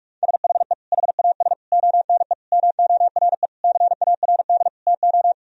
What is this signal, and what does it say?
Morse code, heard as a keyed tone, about 46 wpm, 700 Hz. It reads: SHE HAS ONE MORE CARD TO
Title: Sentences from Top 1000 Words - Review Part 4 - 45wpm